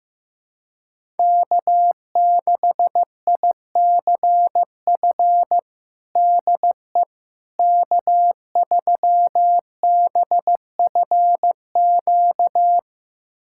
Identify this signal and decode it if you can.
Morse code, heard as a keyed tone, about 15 wpm, 700 Hz. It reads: K6ICF DE K3BFQ
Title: USA Callsign DE Callsign - 5 Characters - 15wpm